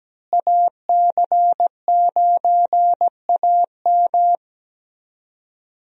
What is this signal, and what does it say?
Morse code, heard as a keyed tone, about 17 wpm, 700 Hz. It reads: AC9AM
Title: USA Call Signs - 5 Characters - Repeated 2x - 17wpm